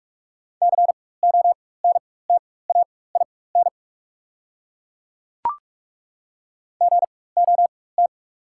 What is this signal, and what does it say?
Morse code, heard as a keyed tone, about 43 wpm, spaced as for 21 wpm, 690 Hz. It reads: CONTAIN GOT
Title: 4x Character Spacing - Top 500 Words - 45wpm